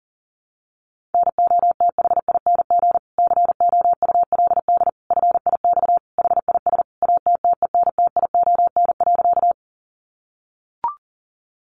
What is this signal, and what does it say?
Morse code, heard as a keyed tone, about 40 wpm, 700 Hz. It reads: NOTHING COULD FIX HIS ATTENTION.
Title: Sentences from Top 1400 Words 40wpm